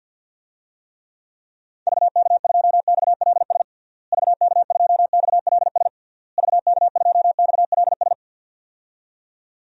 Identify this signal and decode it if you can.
Morse code, heard as a keyed tone, about 49 wpm, 700 Hz. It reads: VK2XLS VK2XLS VK2XLS
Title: DXCC-100 Call Signs - 6 Characters - Repeated 3x - 50wpm